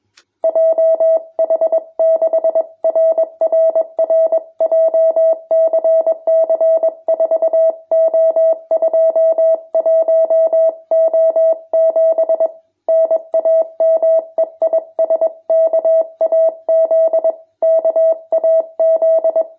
Morse code, heard as a keyed tone, about 21 wpm, 650 Hz. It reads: JH6RRR JCC 4O21O7 NAME IS KAZ KAZ
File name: jh8cbh / 2021cw